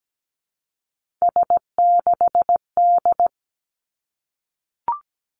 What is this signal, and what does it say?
Morse code, heard as a keyed tone, about 17 wpm, 700 Hz. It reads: S6D
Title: Three Challenging Letter-Number Combinations 17wpm